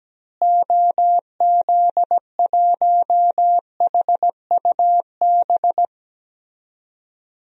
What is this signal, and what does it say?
Morse code, heard as a keyed tone, about 17 wpm, 700 Hz. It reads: OZ1HUB
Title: DXCC-100 Call Signs - 6 Characters - Repeated 2x - 17wpm